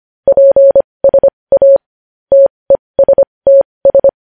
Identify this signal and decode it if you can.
Morse code, heard as a keyed tone, about 25 wpm, spaced as for 20 wpm, 550 Hz. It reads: PSA TESTS
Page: news-headlines-in-morse-code-at-20-wpm